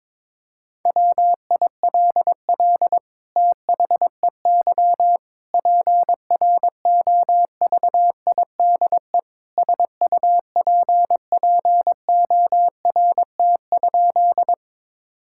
Morse code, heard as a keyed tone, about 22 wpm, 700 Hz. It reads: WILL THEY PROVIDE SUPPORT?
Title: Sentences from Top 1000 Words - Review Part 5 - 22wpm